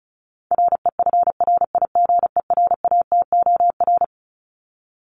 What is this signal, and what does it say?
Morse code, heard as a keyed tone, about 34 wpm, 700 Hz. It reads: REFRIGERATOR
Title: Top 1501-1750 Words 35wpm